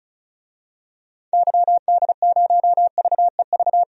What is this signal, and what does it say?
Morse code, heard as a keyed tone, about 34 wpm, 700 Hz. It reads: YD0VEV
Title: DXCC-100 Call Signs - 6 Characters - 35wpm